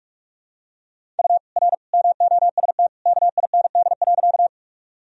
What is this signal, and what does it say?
Morse code, heard as a keyed tone, about 42 wpm, 700 Hz. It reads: U R MOST KIND.
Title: Sentences from Top 200 Words and Abbreviations 45wpm